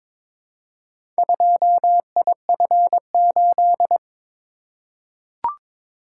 Morse code, heard as a keyed tone, about 22 wpm, 700 Hz. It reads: 2IF8
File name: practice / 22wpm / Four Letter-Number Combinations - 22wpm